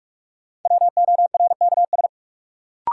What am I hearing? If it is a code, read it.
Morse code, heard as a keyed tone, about 46 wpm, 700 Hz. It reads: WORKS
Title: Sentences from Top 1000 Words - Review Part 5 - New Word Drill - 45wpm